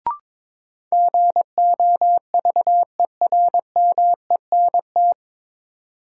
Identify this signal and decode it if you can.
Morse code, heard as a keyed tone, about 21 wpm, 700 Hz. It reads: GOVERMENT
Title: Sentences from Top 1300 Words - New Word Drill Alt - 22wpm